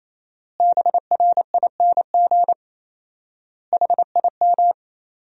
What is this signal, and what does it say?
Morse code, heard as a keyed tone, about 28 wpm, 700 Hz. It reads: BRING HIM
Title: 4x Word-Spacing - Sentences from Top 500 Words - 28wpm